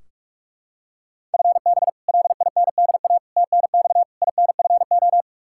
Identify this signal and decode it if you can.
Morse code, heard as a keyed tone, about 44 wpm, 700 Hz. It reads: FB LINDA TNX INFO
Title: Practice QSO - Exchange 3 - More Info - 45wpm